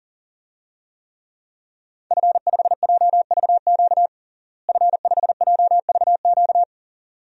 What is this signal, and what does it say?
Morse code, heard as a keyed tone, about 40 wpm, 700 Hz. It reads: F5JVQ F5JVQ
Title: DXCC-100 Call Signs - 5 Characters - Repeated 2x - 40wpm